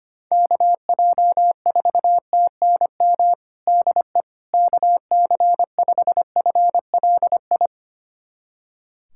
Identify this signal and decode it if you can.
Morse code, heard as a keyed tone, about 25 wpm, 700 Hz. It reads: KJ4TNM DE KC5FLI
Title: USA Callsign DE Callsign - 6 Characters - 25wpm